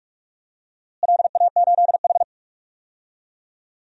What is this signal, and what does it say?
Morse code, heard as a keyed tone, about 44 wpm, 700 Hz. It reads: LA8H
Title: DXCC-100 Call Signs - 4 Characters - Repeated 3x - 45wpm